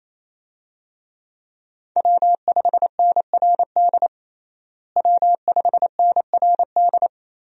Morse code, heard as a keyed tone, about 28 wpm, 700 Hz. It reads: W5NRD W5NRD
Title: USA Call Signs - 5 Characters - Repeated 2x - 28wpm